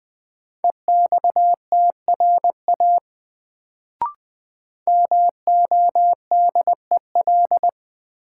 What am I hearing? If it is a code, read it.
Morse code, heard as a keyed tone, about 20 wpm, 700 Hz. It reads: EXTRA MODEL
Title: Sentences from Top 1000 Words - New Word Drill - 20wpm